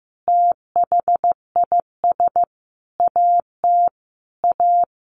Morse code, heard as a keyed tone, about 15 wpm, 700 Hz. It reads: THIS AT A
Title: Sentences from Top 1000 Words 15wpm